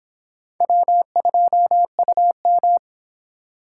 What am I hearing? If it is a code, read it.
Morse code, heard as a keyed tone, about 25 wpm, 700 Hz. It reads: W2UM